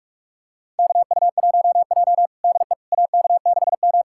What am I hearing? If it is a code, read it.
Morse code, heard as a keyed tone, about 41 wpm, 700 Hz. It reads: KU1J DE AK6M
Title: USA Callsign DE Callsign - 4 Characters - 45wpm